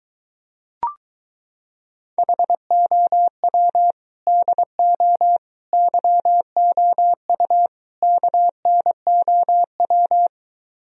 Morse code, heard as a keyed tone, about 24 wpm, 700 Hz. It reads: HOW DO YOU KNOW